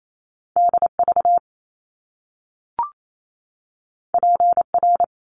Morse code, heard as a keyed tone, about 28 wpm, 700 Hz. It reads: DV PR
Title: Top Two Letter Combinations in Words 28wpm